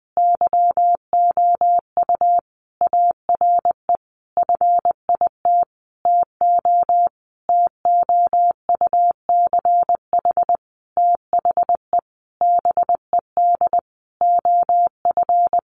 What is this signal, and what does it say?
Morse code, heard as a keyed tone, about 20 wpm, 700 Hz. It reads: YOU ARE FIT TO TOUCH THE BED OF
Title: Sentences from Top 1200 Words 20wpm